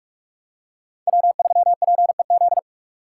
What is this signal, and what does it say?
Morse code, heard as a keyed tone, about 41 wpm, 700 Hz. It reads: W3PEZ
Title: USA Call Signs - 5 Characters - 6x Character Spacing - 45wpm